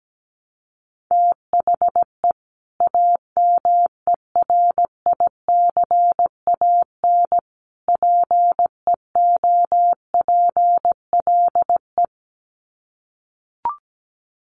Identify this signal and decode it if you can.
Morse code, heard as a keyed tone, about 17 wpm, 700 Hz. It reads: THE AMERICAN PEOPLE
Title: Sets of 3 Words 17wpm